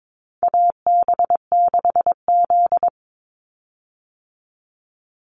Morse code, heard as a keyed tone, about 22 wpm, 700 Hz. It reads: AB6Z
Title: Call Signs Repeated 2x - 22wpm